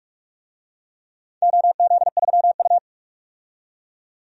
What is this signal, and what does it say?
Morse code, heard as a keyed tone, about 46 wpm, 700 Hz. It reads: OZ3U
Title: DXCC-100 Call Signs - 4 Characters - 45wpm